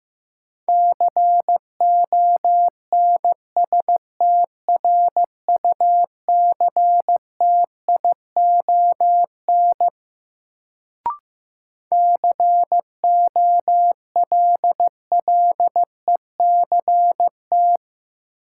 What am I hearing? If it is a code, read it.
Morse code, heard as a keyed tone, about 16 wpm, 700 Hz. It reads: CONSTRUCTION COLLECT
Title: Sentences from Top 1600 Words - New Word Drill Alt - 15wpm